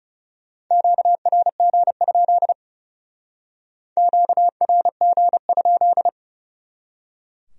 Morse code, heard as a keyed tone, about 38 wpm, 700 Hz. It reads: QRG? QRG?